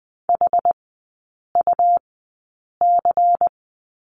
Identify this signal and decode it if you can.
Morse code, heard as a keyed tone, about 20 wpm, spaced as for 8 wpm, 690 Hz. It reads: HUC
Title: 159 Lesson 26e - Letter g - Three Characters 20wpm